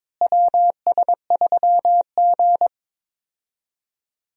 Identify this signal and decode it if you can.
Morse code, heard as a keyed tone, about 22 wpm, 700 Hz. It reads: WS3G
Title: USA Call Signs - 4 Characters - Repeated 2x - 22wpm